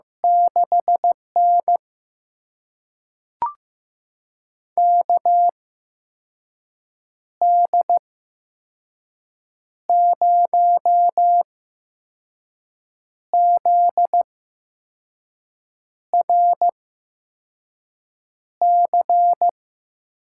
Morse code, heard as a keyed tone, about 16 wpm, spaced as for 5 wpm, 700 Hz. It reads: 6N KD0ZRC
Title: USA Call Signs - 8x Character Spacing - 15wpm